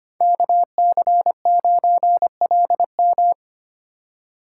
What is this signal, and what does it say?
Morse code, heard as a keyed tone, about 29 wpm, 700 Hz. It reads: KC9LM
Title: Speed Racing - USA 2x2 Call Signs 25wpm